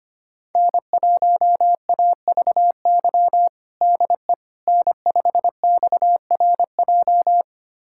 Morse code, heard as a keyed tone, about 25 wpm, 700 Hz. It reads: N1AVY DE N5XRJ
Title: USA Callsign DE Callsign - 5 Characters - 25wpm